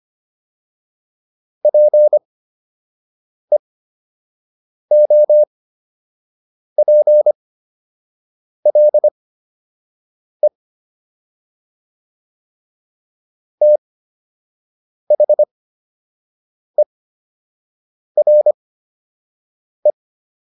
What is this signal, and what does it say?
Morse code, heard as a keyed tone, about 25 wpm, spaced as for 6 wpm, 600 Hz. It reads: PEOPLE THERE